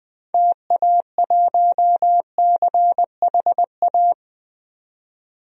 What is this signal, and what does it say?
Morse code, heard as a keyed tone, about 20 wpm, 700 Hz. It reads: TA1CHA
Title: DXCC-100 Call Signs - 6 Characters - 20wpm